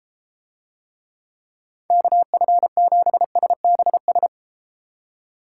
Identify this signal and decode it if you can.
Morse code, heard as a keyed tone, about 32 wpm, 700 Hz. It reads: KF7SBS